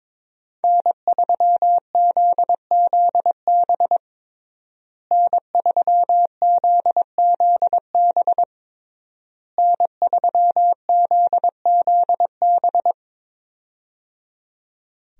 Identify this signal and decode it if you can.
Morse code, heard as a keyed tone, about 22 wpm, 700 Hz. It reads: N3ZZB N3ZZB N3ZZB
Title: USA Call Signs - 5 Characters - Repeated 3x - 22wpm